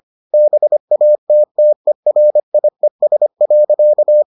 Morse code, heard as a keyed tone, about 25 wpm, 600 Hz. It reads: BATTERIES.